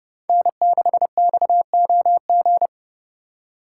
Morse code, heard as a keyed tone, about 35 wpm, 700 Hz. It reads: N6XOG
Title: Speed Racing - USA 1x3 Call Signs 30wpm